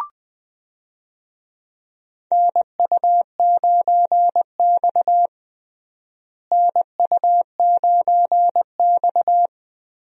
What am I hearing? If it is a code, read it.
Morse code, heard as a keyed tone, about 20 wpm, 700 Hz. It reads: NU9X NU9X
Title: USA Call Signs - 4 Characters - Repeated 3x - 20wpm